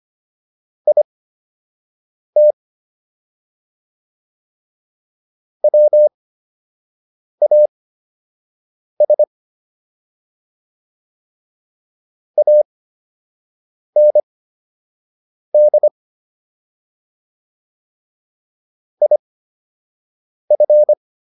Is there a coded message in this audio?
Yes: Morse code from 0.9 to 20.9 s, heard as a keyed tone, about 25 wpm, spaced as for 6 wpm, 600 Hz. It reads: IT WAS AND IF